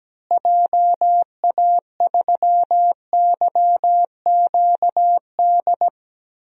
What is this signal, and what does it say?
Morse code, heard as a keyed tone, about 17 wpm, 700 Hz. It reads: JA3YQD
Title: DXCC-100 Call Signs - 6 Characters - Repeated 3x - 17wpm